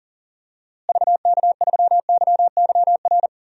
Morse code, heard as a keyed tone, about 40 wpm, 700 Hz. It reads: VK3YYR
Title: DXCC-100 Call Signs - 6 Characters - Repeated 2x - 40wpm